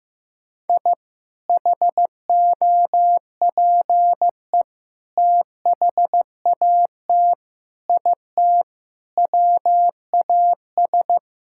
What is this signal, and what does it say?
Morse code, heard as a keyed tone, about 16 wpm, 700 Hz. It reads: I HOPE THAT IT WAS
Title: Sentences from Top 1000 Words - Review Part 5 - 15wpm